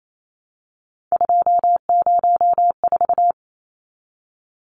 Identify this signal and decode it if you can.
Morse code, heard as a keyed tone, about 28 wpm, 700 Hz. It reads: 204
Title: Numbers - 3-Digit 28wpm